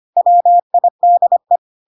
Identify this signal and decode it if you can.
Morse code, heard as a keyed tone, about 25 wpm, 700 Hz. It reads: WIDE
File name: Pinocchio / Pinocchio ch5 25 25 ik1hge